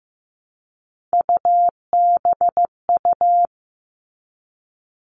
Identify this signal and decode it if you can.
Morse code, heard as a keyed tone, about 15 wpm, 700 Hz. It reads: UBU
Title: All Cut Numbers - 3-Digit 15wpm